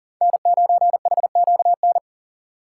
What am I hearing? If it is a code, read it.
Morse code, heard as a keyed tone, about 37 wpm, 700 Hz. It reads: N9HQN
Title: USA Call Signs - 5 Characters - 3x Character Spacing - 40wpm